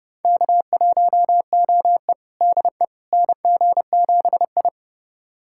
Morse code, heard as a keyed tone, about 31 wpm, 700 Hz. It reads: K1OE DE NG7I
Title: USA Callsign DE Callsign - 4 Characters - 30wpm